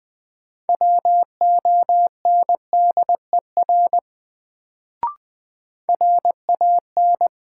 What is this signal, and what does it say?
Morse code, heard as a keyed tone, about 20 wpm, 700 Hz. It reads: WONDER RAN
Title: Sentences from Top 500 Words - New Word Drill - 20wpm